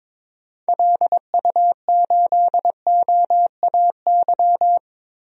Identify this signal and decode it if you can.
Morse code, heard as a keyed tone, about 22 wpm, 700 Hz. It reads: LU8OAY
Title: DXCC-100 Call Signs - 6 Characters - Repeated 3x - 22wpm